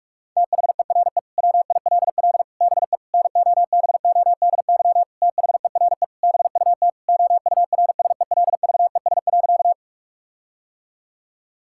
Morse code, heard as a keyed tone, about 41 wpm, 700 Hz. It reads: THERE WILL BE NOBODY THERE BUT OURSELVES.
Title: Sentences from Top 1600 Words 45wpm